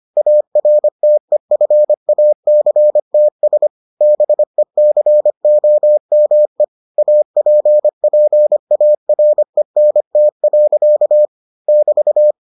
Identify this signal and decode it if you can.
Morse code, heard as a keyed tone, about 25 wpm, 600 Hz. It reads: ARTEFACTS BECOME APPARENT. <BT>